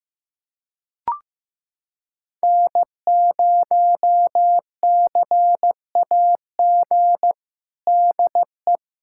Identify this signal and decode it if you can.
Morse code, heard as a keyed tone, about 16 wpm, 700 Hz. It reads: N0CAG DE
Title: USA Callsign DE Callsign - 5 Characters - 15wpm